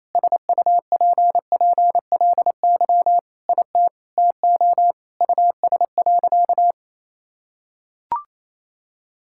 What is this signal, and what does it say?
Morse code, heard as a keyed tone, about 28 wpm, 700 Hz. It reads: SUPPLY IT TO US.
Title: Sentences from Top 1000 Words - Review Part 5 - 28wpm